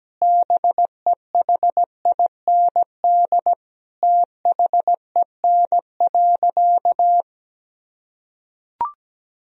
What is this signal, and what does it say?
Morse code, heard as a keyed tone, about 17 wpm, 700 Hz. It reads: BEHIND THEN.
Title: Sentences from Top 1000 Words - Review Part 2 - 17wpm